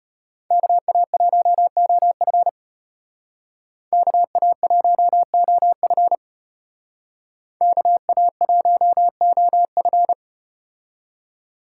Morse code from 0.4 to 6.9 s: KA1OF KA1OF

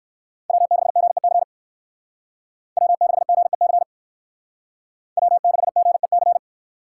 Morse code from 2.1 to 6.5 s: W6GEC W6GEC

